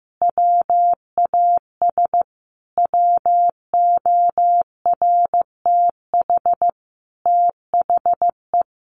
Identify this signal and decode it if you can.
Morse code, heard as a keyed tone, about 15 wpm, 700 Hz. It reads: WAS WORTH THE
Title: Sentences from Top 1500 Words 15wpm